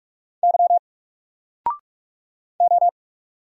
Morse code, heard as a keyed tone, about 44 wpm, 700 Hz. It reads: Y O
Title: Single Character (Letter:Number:Punctuation:Prosign) 45wpm